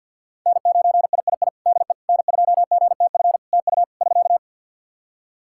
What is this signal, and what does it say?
Morse code, heard as a keyed tone, about 53 wpm, 700 Hz. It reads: N9III DE N2GTF TU <SK>
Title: Practice QSO - Exchange 4 - Ending - 50wpm